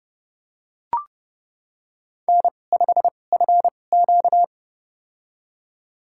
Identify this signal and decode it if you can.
Morse code, heard as a keyed tone, about 30 wpm, spaced as for 22 wpm, 700 Hz. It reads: N5FQ
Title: USA Call Signs - 4 Characters - 2x Character Spacing - 30wpm